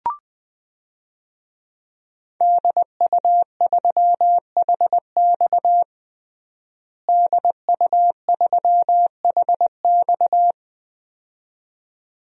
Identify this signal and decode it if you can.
Morse code, heard as a keyed tone, about 20 wpm, 700 Hz. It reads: DU3HX DU3HX
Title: DXCC-100 Call Signs - 5 Characters - Repeated 2x - 20wpm